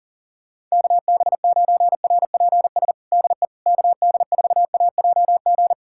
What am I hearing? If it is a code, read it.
Morse code, heard as a keyed tone, about 40 wpm, 700 Hz. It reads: KB9RPS DE KD4AJG
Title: USA Callsign DE Callsign - 6 Characters - 40wpm